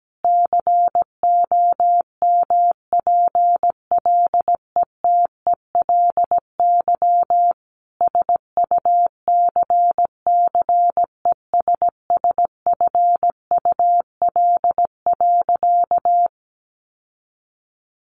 Morse code, heard as a keyed tone, about 17 wpm, 700 Hz. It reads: COMPLETELY SUCCESSFUL.
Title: Sentences from Top 1100 Words 17wpm